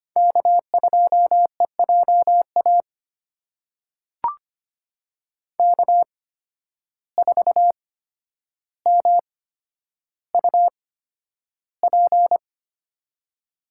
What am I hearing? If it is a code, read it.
Morse code, heard as a keyed tone, about 25 wpm, spaced as for 11 wpm, 700 Hz. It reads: K2EJA K4MUP
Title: USA Call Signs - 5 Characters - 8x Character Spacing - 25wpm